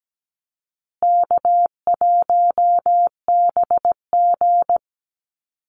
Morse code, heard as a keyed tone, about 17 wpm, 700 Hz. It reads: K1BG
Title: Easy Call Signs Repeated 3x - 17wpm